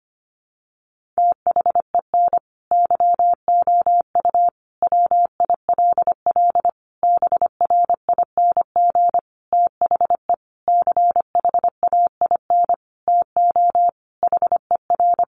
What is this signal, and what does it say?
Morse code, heard as a keyed tone, about 25 wpm, 700 Hz. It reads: THEN YOU WILL BRING THE CHAIN TO HER
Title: Sentences from Top 1500 Words 25wpm